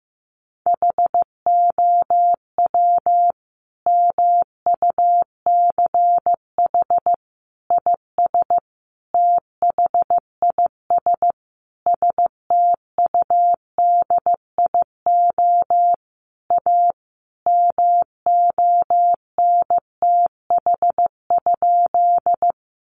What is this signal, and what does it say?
Morse code, heard as a keyed tone, about 15 wpm, 700 Hz. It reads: HOW MUCH IS THIS STUDIO A MONTH?
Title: Sentences from Top 1200 Words 15wpm